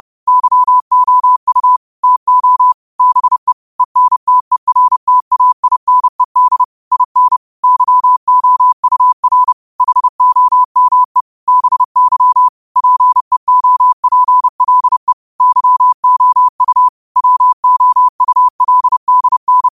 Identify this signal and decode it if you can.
Morse code, heard as a keyed tone, about 30 wpm, 1000 Hz. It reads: YOU TO BE ENTERTAINED IN YOUR HOME BY PEOPLE YOU WOULDN
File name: qotdfiles / 30wpm